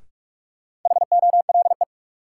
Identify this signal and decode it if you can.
Morse code, heard as a keyed tone, about 46 wpm, 700 Hz. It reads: HOLE
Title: Sentences from Top 1600 Words - New Word Drill Alt - 45wpm